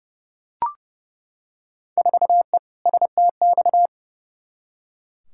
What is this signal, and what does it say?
Morse code, heard as a keyed tone, about 30 wpm, 700 Hz. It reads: 4E STX
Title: Field Day Exchanges Encore - 30wpm